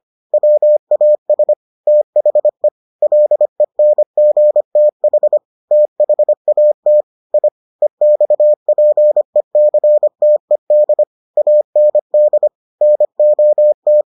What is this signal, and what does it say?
Morse code, heard as a keyed tone, about 25 wpm, 600 Hz. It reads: WAS THE LENGTH THAT I EXPECTED AND NOT